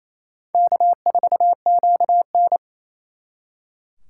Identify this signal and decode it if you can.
Morse code, heard as a keyed tone, about 35 wpm, 700 Hz. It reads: K4QN
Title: Speed Racing - USA Call Signs 28wpm